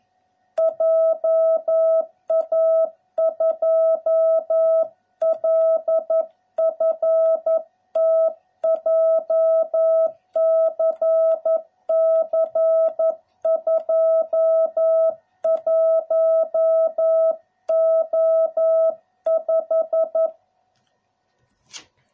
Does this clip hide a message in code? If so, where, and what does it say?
Yes: Morse code from 0.6 to 20.3 s, heard as a keyed tone, about 11 wpm, 660 Hz. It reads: JA2LFT JCC 21O5